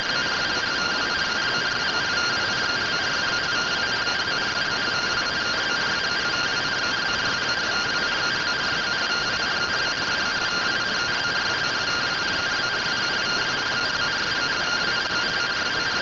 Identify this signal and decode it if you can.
Two-tone teleprinter signal, about 170 Hz shift.